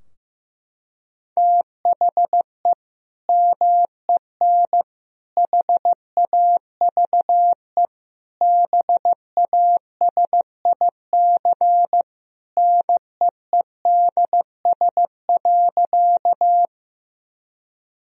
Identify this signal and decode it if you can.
Morse code, heard as a keyed tone, about 15 wpm, 700 Hz. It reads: THE MEN HAVE BASIC NEEDS.
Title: Sentences from Top 1000 Words - Review Part 3 - 15wpm